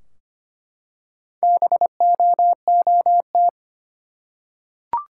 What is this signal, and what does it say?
Morse code, heard as a keyed tone, about 25 wpm, 700 Hz. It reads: BOOT
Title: Four Letter Words - 25wpm